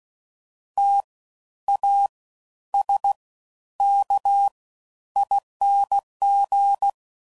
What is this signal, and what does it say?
Morse code, heard as a keyed tone, about 16 wpm, spaced as for 10 wpm, 800 Hz. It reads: TASKING
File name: Suffixes / Special INGs / Fast Ings